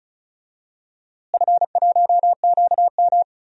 Morse code, heard as a keyed tone, about 36 wpm, 700 Hz. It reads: F1QM
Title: DXCC-100 Call Signs - 4 Characters - Repeated 2x - 35wpm